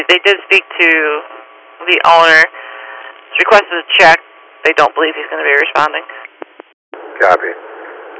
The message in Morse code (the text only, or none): none